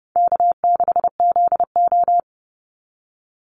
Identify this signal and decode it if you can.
Morse code, heard as a keyed tone, about 30 wpm, 700 Hz. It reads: K6ZO
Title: Easy Call Signs Repeated 2x - 30wpm